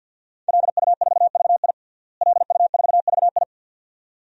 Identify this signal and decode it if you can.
Morse code, heard as a keyed tone, about 52 wpm, 700 Hz. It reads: LU4VI LU4VI
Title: DXCC-100 Call Signs - 5 Characters - Repeated 2x - 50wpm